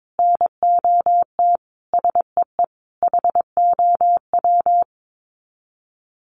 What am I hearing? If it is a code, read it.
Morse code, heard as a keyed tone, about 22 wpm, 700 Hz. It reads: NOT SEE HOW
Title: Sets of 5 Words 22wpm